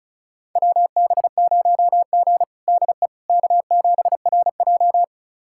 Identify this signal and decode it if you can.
Morse code, heard as a keyed tone, about 36 wpm, 700 Hz. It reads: WB0G DE K7RJ
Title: USA Callsign DE Callsign - 4 Characters - 35wpm